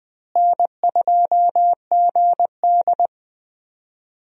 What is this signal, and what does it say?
Morse code, heard as a keyed tone, about 20 wpm, 700 Hz. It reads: N2GD
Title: USA Call Signs - 4 Characters - Repeated 2x - 20wpm